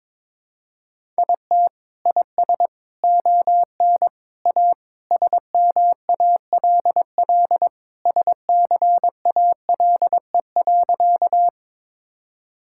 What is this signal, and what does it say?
Morse code, heard as a keyed tone, about 22 wpm, 700 Hz. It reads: IT IS ON A SMALL SCALE.
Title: Sentences from Top 1000 Words - Review Part 5 - 22wpm